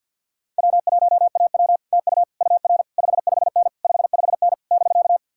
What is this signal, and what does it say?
Morse code, heard as a keyed tone, about 64 wpm, 700 Hz. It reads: W1AW TU UR 55N 55N <BK>